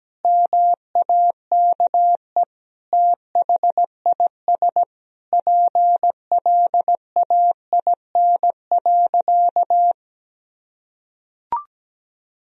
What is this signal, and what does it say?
Morse code, heard as a keyed tone, about 17 wpm, 700 Hz. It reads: MAKE THIS PLAIN.
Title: Sentences from Top 1000 Words - Review Part 4 - 17wpm